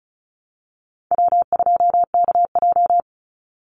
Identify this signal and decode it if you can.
Morse code, heard as a keyed tone, about 34 wpm, 700 Hz. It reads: W2KJ
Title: Call Signs - 35wpm